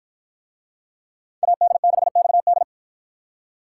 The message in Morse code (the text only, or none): AD6XD